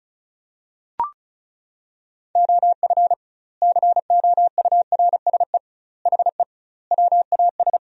Morse code, heard as a keyed tone, about 36 wpm, 700 Hz. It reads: OF COURSE HE WAS
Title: 2x Word-Spacing - Sentences from Top 500 Words - 35wpm